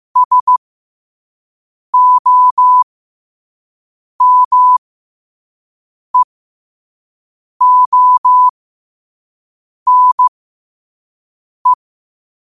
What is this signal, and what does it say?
Morse code, heard as a keyed tone, about 15 wpm, spaced as for 5 wpm, 1000 Hz. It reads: SOMEONE
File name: qotdfiles / 5wpm